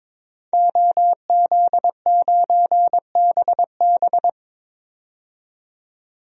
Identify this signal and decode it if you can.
Morse code, heard as a keyed tone, about 21 wpm, 700 Hz. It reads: OZ9BB
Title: DXCC-100 Call Signs - 5 Characters - 22wpm